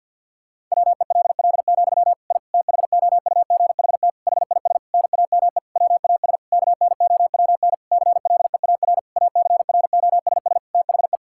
Morse code, heard as a keyed tone, about 51 wpm, 700 Hz. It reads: WELL, I THOUGHT HIS NAME WAS KNOWN CLEAR ACROSS THE
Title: Sentences from Top 1000 Words - Review Part 5 - 50wpm